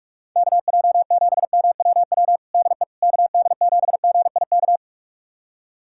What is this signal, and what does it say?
Morse code, heard as a keyed tone, about 46 wpm, 700 Hz. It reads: KJ7MWW DE KD7GIK
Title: USA Callsign DE Callsign - 6 Characters - 45wpm